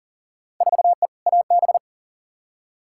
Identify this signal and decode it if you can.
Morse code, heard as a keyed tone, about 40 wpm, 700 Hz. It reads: 4E AB
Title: Field Day Exchanges Encore - 40wpm